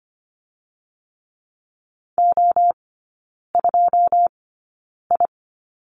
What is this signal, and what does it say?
Morse code, heard as a keyed tone, about 25 wpm, spaced as for 9 wpm, 690 Hz. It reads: O2I
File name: Lesson 14e - Number 2 - Three Characters 25wpm